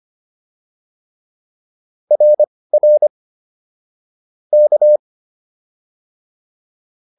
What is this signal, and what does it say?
Morse code, heard as a keyed tone, about 25 wpm, spaced as for 18 wpm, 600 Hz. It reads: RR K